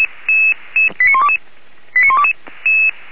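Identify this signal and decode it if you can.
Five-tone selective call (ZVEI), heard as quick runs of five tones, tones of about 80 ms. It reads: 89122 89122
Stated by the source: ZVEI five-tone call received with an SDR.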